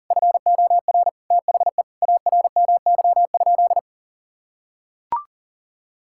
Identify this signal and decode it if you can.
Morse code, heard as a keyed tone, about 40 wpm, 700 Hz. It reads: FOR THE ARMY?
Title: Sentences from Top 1000 Words - Review Part 5 - 40wpm